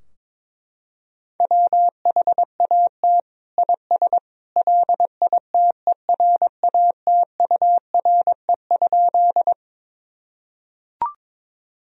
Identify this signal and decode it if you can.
Morse code, heard as a keyed tone, about 22 wpm, 700 Hz. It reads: WHAT IS LITERATURE?
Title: Sentences from Top 1000 Words - Review Part 3 - 22wpm